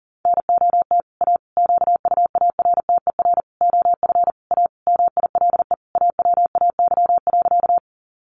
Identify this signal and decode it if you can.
Morse code, heard as a keyed tone, about 37 wpm, 700 Hz. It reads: NOT A QUARTER OF A MILE AWAY.
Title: Sentences from Top 1300 Words 40wpm